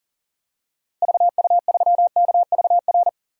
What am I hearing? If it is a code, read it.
Morse code, heard as a keyed tone, about 40 wpm, 700 Hz. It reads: VU3KVR
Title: DXCC-100 Call Signs - 6 Characters - Repeated 3x - 40wpm